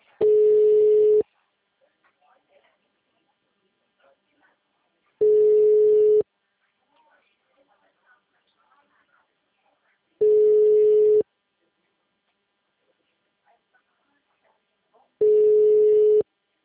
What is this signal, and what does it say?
Ringback tone (Europe, 425 Hz), with speech.